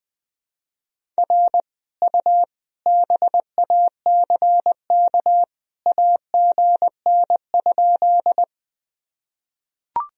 Morse code, heard as a keyed tone, about 21 wpm, 700 Hz. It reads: R U BACK AGN?
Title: Sentences from Top 200 Words and Abbreviations 20wpm